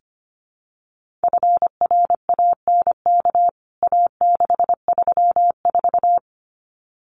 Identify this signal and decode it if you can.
Morse code, heard as a keyed tone, about 25 wpm, 700 Hz. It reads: FRANK A634
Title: CWT Exchange 25wpm